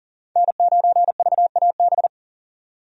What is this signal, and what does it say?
Morse code, heard as a keyed tone, about 37 wpm, 700 Hz. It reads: N9VAB
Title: USA Call Signs - 5 Characters - 40wpm